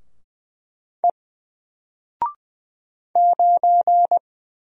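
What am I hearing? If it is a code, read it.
Morse code, heard as a keyed tone, about 20 wpm, 700 Hz. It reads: E 9
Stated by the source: Single letter and number Morse drill at 20 wpm, from N4LCD.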